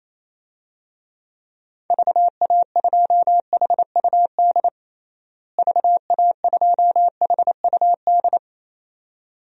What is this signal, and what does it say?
Morse code, heard as a keyed tone, about 28 wpm, 700 Hz. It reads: VA2HUD VA2HUD
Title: DXCC-100 Call Signs - 6 Characters - Repeated 2x - 28wpm